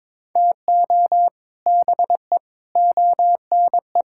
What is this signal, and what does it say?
Morse code, heard as a keyed tone, about 21 wpm, 700 Hz. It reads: TO BE ONE
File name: practice / 22wpm / Build A Phrase - 5-Word Phrases - 22wpm